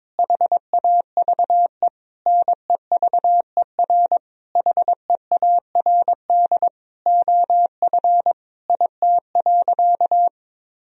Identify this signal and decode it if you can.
Morse code, heard as a keyed tone, about 22 wpm, 700 Hz. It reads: HAVE NEVER HEARD OF IT.
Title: Sentences from Top 1000 Words - Review Part 3 - 22wpm